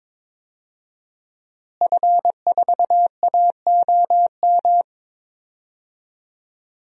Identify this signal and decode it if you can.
Morse code, heard as a keyed tone, about 22 wpm, 700 Hz. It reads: F4AOM
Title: DXCC-100 Call Signs - 5 Characters - Repeated 2x - 22wpm